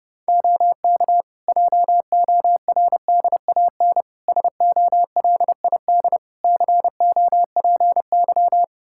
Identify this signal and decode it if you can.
Morse code, heard as a keyed tone, about 32 wpm, 700 Hz. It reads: OK JORDAN SOLID COPY
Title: Practice QSO - Exchange 3 - More Info - 30wpm